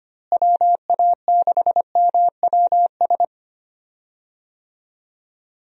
Morse code, heard as a keyed tone, about 25 wpm, 700 Hz. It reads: WA6MWS
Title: USA Call Signs - 6 Characters - Repeated 3x - 25wpm